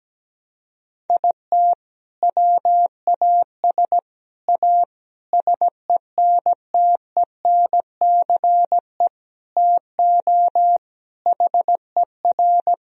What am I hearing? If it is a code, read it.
Morse code, heard as a keyed tone, about 17 wpm, 700 Hz. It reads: IT WAS A SENTENCE TO HER
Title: Sentences from Top 1000 Words - Review Part 1 - 17wpm